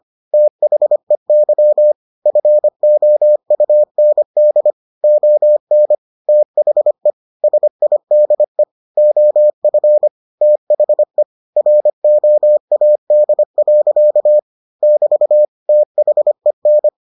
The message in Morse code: THEY FOUND ON THE SIDE OF THE ROAD. <BT> THEN

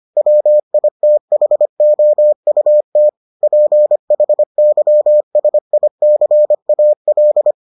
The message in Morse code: WITHOUT PHYSICAL

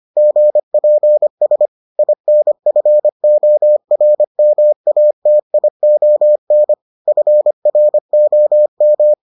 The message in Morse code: GPS INFORMATION FROM

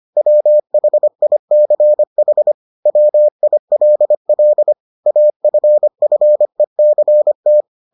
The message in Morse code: WHICH WILL AFFECT

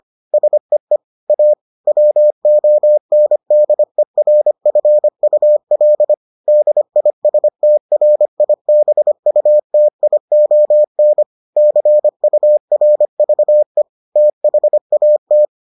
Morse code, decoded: SEE A WONDERFUL DISTRIBUTION CURVE THAT

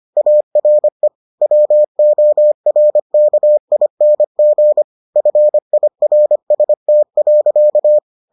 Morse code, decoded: ARE WORKING FIRST.

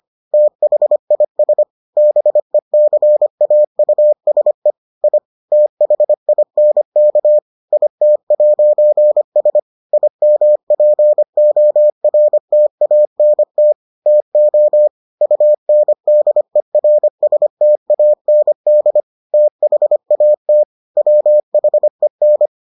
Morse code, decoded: THIS BECAUSE I THINK IT'S IMPORTANT TO UNDERSTAND THAT WHEN

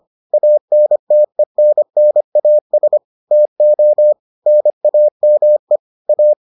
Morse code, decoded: ANTENNAS TO NAME A

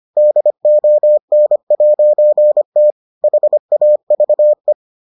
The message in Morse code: DON'T HAVE

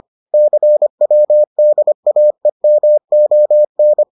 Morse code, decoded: CWDAEMON

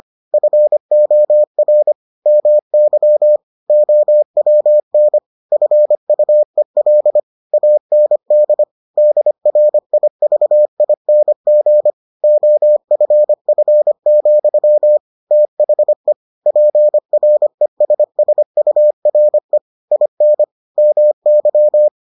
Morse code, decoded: FOR MY OWN FUEL AND DRIVING OFF, THE PRESSURE IN MY

